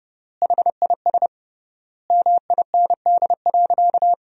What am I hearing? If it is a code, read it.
Morse code, heard as a keyed tone, about 30 wpm, 700 Hz. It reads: HIS MIND.
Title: 3x Word-Spacing - Sentences from Top 500 Words - 30wpm